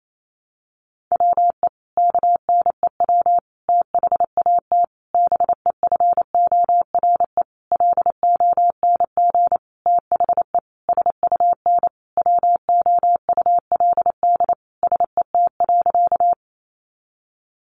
Morse code, heard as a keyed tone, about 28 wpm, 700 Hz. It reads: WE KNEW THAT BEFORE LONG THE SUN WOULD SET.
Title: Sentences from Top 400 Words 28wpm